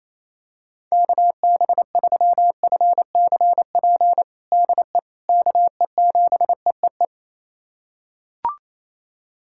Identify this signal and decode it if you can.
Morse code, heard as a keyed tone, about 28 wpm, 700 Hz. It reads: KB3FCP DE KE7EEE
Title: USA Callsign DE Callsign 28wpm